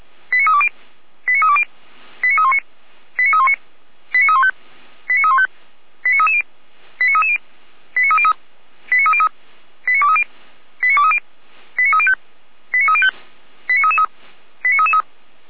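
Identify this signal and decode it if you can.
Five-tone selective call (ZVEI), heard as quick runs of five tones, tones of about 80 ms. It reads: 89320 89320 89319 89319 89315 89315 89330 89330 89303 89303 89230 89230 89305 89305 89333 89333
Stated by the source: ZVEI five-tone selective call received by SDR.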